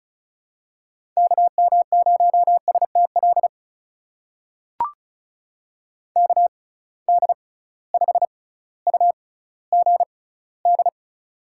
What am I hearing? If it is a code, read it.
Morse code, heard as a keyed tone, about 34 wpm, spaced as for 18 wpm, 700 Hz. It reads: KM0STL KD5UGD
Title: USA Call Signs - 6 Characters - 6x Character Spacing - 35wpm